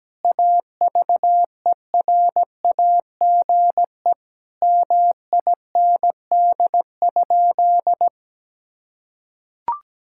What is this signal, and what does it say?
Morse code, heard as a keyed tone, about 17 wpm, 700 Hz. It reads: AVERAGE MIND?
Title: Sentences from Top 1000 Words - Review Part 6 - 17wpm